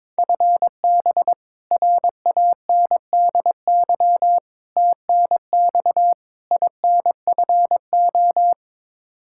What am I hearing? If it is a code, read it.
Morse code, heard as a keyed tone, about 22 wpm, 700 Hz. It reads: FB RANDY TNX INFO
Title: K1USN SST QSO Practice 22wpm